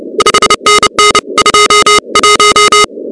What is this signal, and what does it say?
Morse code, heard as a keyed tone, about 30 wpm, 1250 Hz, with noise. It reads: 5NN 21
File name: contests / 2019-cw-wpx-kt5j-k5pi- -2330 0-16